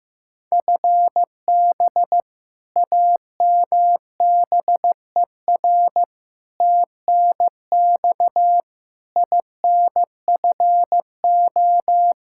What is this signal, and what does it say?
Morse code, heard as a keyed tone, about 16 wpm, 700 Hz. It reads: FB AMBER TNX INFO
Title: Practice QSO - Exchange 3 - More Info - 15wpm